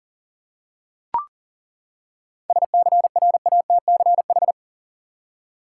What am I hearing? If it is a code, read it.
Morse code, heard as a keyed tone, about 40 wpm, 700 Hz. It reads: SCRATCH
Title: Seven Letter Words - 40wpm